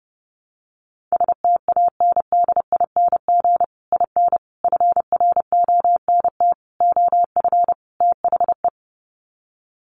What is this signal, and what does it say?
Morse code, heard as a keyed tone, about 30 wpm, 700 Hz. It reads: STANDING IN FRONT OF THE
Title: Sets of 5 Words Encore2 30wpm